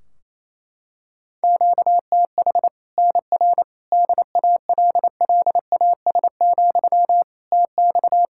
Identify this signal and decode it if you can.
Morse code, heard as a keyed tone, about 28 wpm, 700 Hz. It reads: QTH NR DALLAS, TX
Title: Practice QSO - Exchange 1 - Beginning - 28wpm